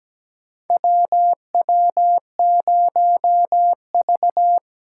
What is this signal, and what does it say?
Morse code, heard as a keyed tone, about 17 wpm, 700 Hz. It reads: WW0V
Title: USA Call Signs - 3x Character Spacing - 17wpm